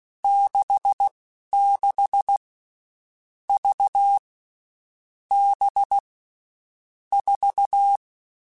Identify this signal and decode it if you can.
Morse code, heard as a keyed tone, about 16 wpm, spaced as for 11 wpm, 800 Hz. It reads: 66 V B 4
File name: Groups@10 / Groups@10 mp3s / VB46@10